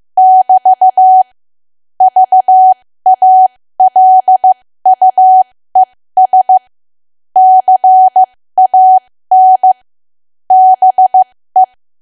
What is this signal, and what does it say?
Morse code, heard as a keyed tone, about 15 wpm, spaced as for 13 wpm, 750 Hz. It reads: <BT> VALUES CAN BE
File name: cwa / W1AW-Level2 / 13WPM